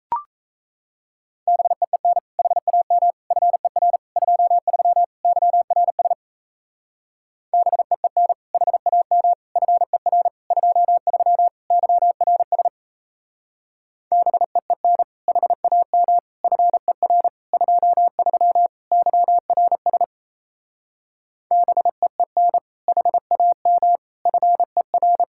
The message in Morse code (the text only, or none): BEEN HAM FER 23 YRS BEEN HAM FER 23 YRS BEEN HAM FER 23 YRS BEEN HAM FER